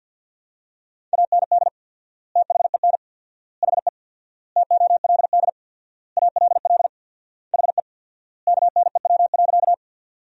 Morse code, heard as a keyed tone, about 53 wpm, 700 Hz. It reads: AND THEN HE TOLD ALL HE KNEW.
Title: 4x Word-Spacing - Sentences from Top 400 Words - 50wpm